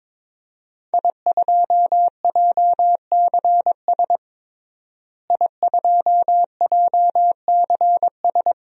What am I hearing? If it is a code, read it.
Morse code, heard as a keyed tone, about 22 wpm, 700 Hz. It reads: I2JCS I2JCS
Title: DXCC-100 Call Signs - 5 Characters - Repeated 2x - 22wpm